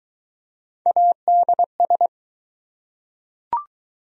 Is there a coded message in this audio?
Yes: Morse code from 0.9 to 2.1 s, heard as a keyed tone, about 20 wpm, 700 Hz. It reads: ADS